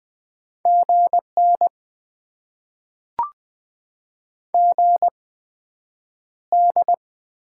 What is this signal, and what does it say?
Morse code, heard as a keyed tone, about 20 wpm, spaced as for 9 wpm, 700 Hz. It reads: GN GD
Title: Top 90 QSO Elements - 8x Character Spacing - 20wpm